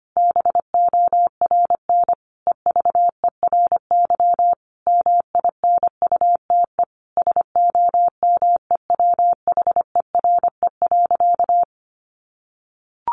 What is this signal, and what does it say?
Morse code, heard as a keyed tone, about 25 wpm, 700 Hz. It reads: BORN EVERY MINUTE SOMEWHERE.
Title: Sentences from Top 1400 Words 25wpm